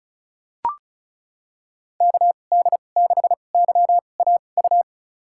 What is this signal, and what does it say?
Morse code, heard as a keyed tone, about 34 wpm, spaced as for 25 wpm, 700 Hz. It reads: KD6YAU
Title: USA Call Signs - 2x Character Spacing - 35wpm